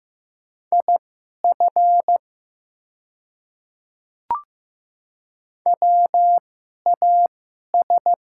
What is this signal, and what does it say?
Morse code, heard as a keyed tone, about 15 wpm, spaced as for 11 wpm, 700 Hz. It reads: IF WAS
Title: 2x Character Spacing - Top 200 Words - 15wpm